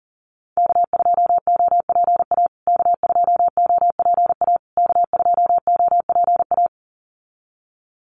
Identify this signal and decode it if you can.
Morse code, heard as a keyed tone, about 40 wpm, 700 Hz. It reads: K2OPA K2OPA K2OPA
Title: USA Call Signs Repeated 3x 40wpm